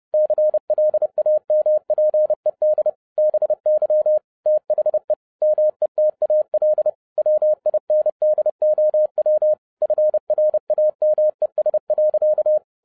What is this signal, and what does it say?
Morse code, heard as a keyed tone, about 30 wpm, 610 Hz. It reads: CLAMPED BY THE METAL WINDOW FRAMES.